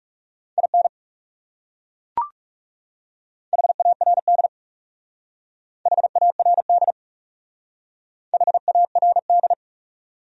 Morse code, heard as a keyed tone, about 36 wpm, 700 Hz. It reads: IN HARD HARD HARD
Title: Speed-Racing Top 300 Words 30wpm